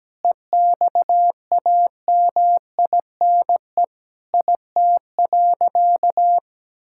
Morse code, heard as a keyed tone, about 17 wpm, 700 Hz. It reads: EXAMINE IT.
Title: Sentences from Top 1600 Words 17wpm